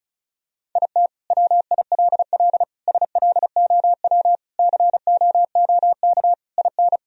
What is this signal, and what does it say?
Morse code, heard as a keyed tone, about 34 wpm, 700 Hz. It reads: IT WILL SLOW COOK IN
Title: Sentences from Top 1000 Words - Review Part 4 - 35wpm